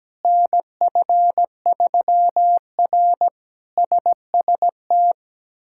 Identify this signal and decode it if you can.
Morse code, heard as a keyed tone, about 17 wpm, 700 Hz. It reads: NF3R SST
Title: K1USN SST QSO Practice - 17wpm